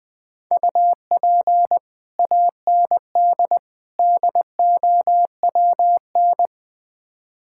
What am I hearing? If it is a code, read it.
Morse code, heard as a keyed tone, about 20 wpm, 700 Hz. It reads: UP AND DOWN
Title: Build A Phrase - 4-Word Phrases - 20wpm